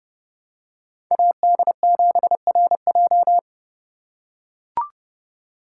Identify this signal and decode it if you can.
Morse code, heard as a keyed tone, about 28 wpm, 700 Hz. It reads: AD7RJ